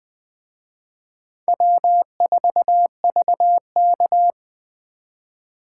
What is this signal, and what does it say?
Morse code, heard as a keyed tone, about 20 wpm, 700 Hz. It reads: W4VK
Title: Speed Racing - USA 1x2 Call Signs 17wpm